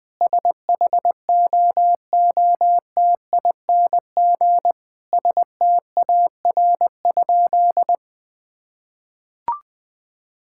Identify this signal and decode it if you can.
Morse code, heard as a keyed tone, about 20 wpm, 700 Hz. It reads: SHOOTING STAR?
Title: Sentences with 8-Word Phrases - 20wpm